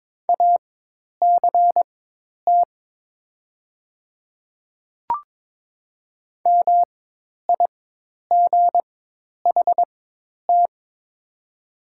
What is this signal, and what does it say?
Morse code, heard as a keyed tone, about 22 wpm, spaced as for 10 wpm, 700 Hz. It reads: ACT MIGHT
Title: 4x Character Spacing - Top 300 Words - 22wpm